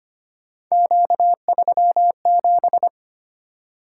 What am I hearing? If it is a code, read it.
Morse code, heard as a keyed tone, about 25 wpm, 700 Hz. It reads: Q37
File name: Three Letter-Number Combinations - 25wpm